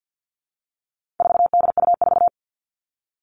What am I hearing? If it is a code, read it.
Morse code, heard as a keyed tone, about 50 wpm, 700 Hz. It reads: 4DU4
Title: All Cut Numbers - 4-Digit 50wpm